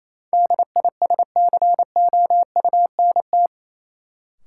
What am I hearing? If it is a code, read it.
Morse code, heard as a keyed tone, about 28 wpm, 700 Hz. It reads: DISCOUNT
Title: Sentences from Top 1200 Words - New Word Drill Alt - 28wpm